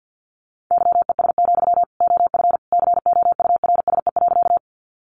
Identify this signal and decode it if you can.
Morse code, heard as a keyed tone, about 50 wpm, 700 Hz. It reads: YES, OF COURSE.
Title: Sentences from Top 500 Words 50wpm